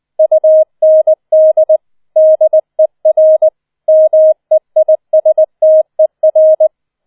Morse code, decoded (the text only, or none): UND DER MEISTER